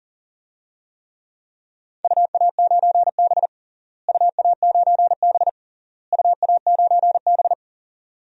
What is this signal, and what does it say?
Morse code, heard as a keyed tone, about 40 wpm, 700 Hz. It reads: UA9B UA9B UA9B
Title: DXCC-100 Call Signs - 4 Characters - Repeated 3x - 40wpm